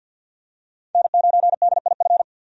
Morse code, heard as a keyed tone, about 51 wpm, 700 Hz. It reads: N9DIF
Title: USA Call Signs - 5 Characters - 50wpm